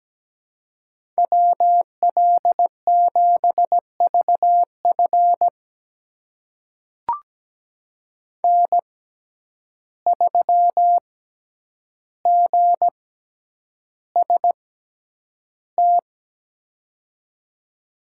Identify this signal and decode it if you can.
Morse code, heard as a keyed tone, about 17 wpm, spaced as for 9 wpm, 700 Hz. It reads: WL7VF N3GST
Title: USA Call Signs - 5 Characters - 6x Character Spacing - 17wpm